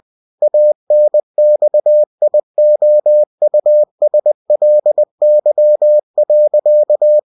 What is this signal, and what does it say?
Morse code, heard as a keyed tone, about 20 wpm, 600 Hz. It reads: ANXIOUSLY.
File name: cw-books / winnie-the-pooh / winnie-20wpm-page-0079